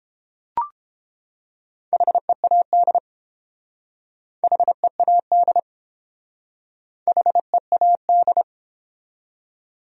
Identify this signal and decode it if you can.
Morse code, heard as a keyed tone, about 31 wpm, 700 Hz. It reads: HEAD HEAD HEAD